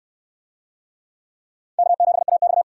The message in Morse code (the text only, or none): K7AX